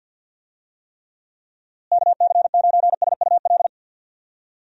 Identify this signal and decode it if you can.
Morse code, heard as a keyed tone, about 52 wpm, 700 Hz. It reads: KC9SUL
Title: USA Call Signs - 6 Characters - Repeated 2x - 50wpm